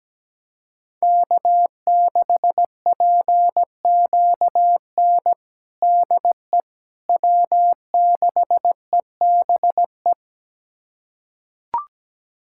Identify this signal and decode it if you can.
Morse code, heard as a keyed tone, about 17 wpm, 700 Hz. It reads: K6PQN DE W6EBE
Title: USA Callsign DE Callsign - 5 Characters - 17wpm